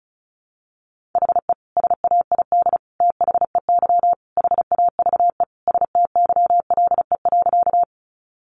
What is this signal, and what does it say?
Morse code, heard as a keyed tone, about 34 wpm, 700 Hz. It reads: HE SAID THEY HAVE STYLE.
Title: Sentences from Top 800 Words 35wpm